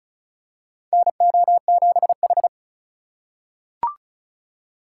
Morse code, heard as a keyed tone, about 36 wpm, 700 Hz. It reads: NO7H
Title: USA Call Signs - 2x Character Spacing - 35wpm